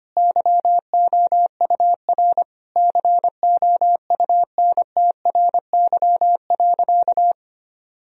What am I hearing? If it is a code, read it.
Morse code, heard as a keyed tone, about 25 wpm, 700 Hz. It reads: YOUR COUNTRY.
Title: Sentences from Top 1000 Words - Review Part 3 - 25wpm